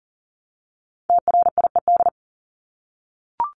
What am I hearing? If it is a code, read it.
Morse code, heard as a keyed tone, about 40 wpm, 700 Hz. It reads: TRIED
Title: ED Suffix - Top 100 Words with Base Words 40wpm